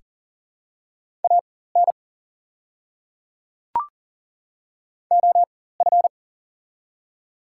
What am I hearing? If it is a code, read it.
Morse code, heard as a keyed tone, about 40 wpm, spaced as for 19 wpm, 690 Hz. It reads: AN OF
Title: 4x Character Spacing - Top 300 Words - 40wpm